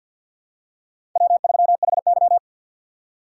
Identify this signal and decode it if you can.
Morse code, heard as a keyed tone, about 51 wpm, 700 Hz. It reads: W3HY
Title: USA Call Signs - 4 Characters - 3x Character Spacing - 50wpm